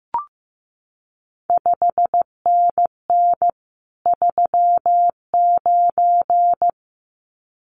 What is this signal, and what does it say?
Morse code, heard as a keyed tone, about 15 wpm, 700 Hz. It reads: 5NN 39
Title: CQ WW DX Contest Exchange - 15wpm